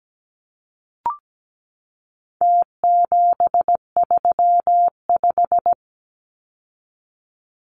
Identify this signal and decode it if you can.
Morse code, heard as a keyed tone, about 17 wpm, 700 Hz. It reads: T735
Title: Numbers - Contest Style - 4-Digit 17wpm